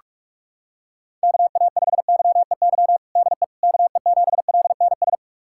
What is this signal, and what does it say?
Morse code, heard as a keyed tone, about 46 wpm, 700 Hz. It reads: KA5YEY DE KE7LNS
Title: USA Callsign DE Callsign 45wpm